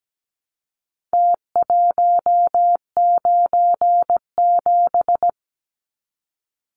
Morse code, heard as a keyed tone, about 17 wpm, 700 Hz. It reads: T197
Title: Numbers - Contest Style - 4-Digit 17wpm